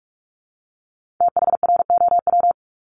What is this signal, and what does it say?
Morse code, heard as a keyed tone, about 46 wpm, 700 Hz. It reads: THROW
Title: Multi-letter Phonemes 45wpm